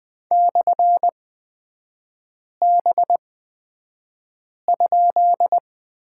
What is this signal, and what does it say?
Morse code, heard as a keyed tone, about 20 wpm, spaced as for 5 wpm, 700 Hz. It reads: /B?
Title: 236 Lesson 38e - Letter q - Three Characters 20wpm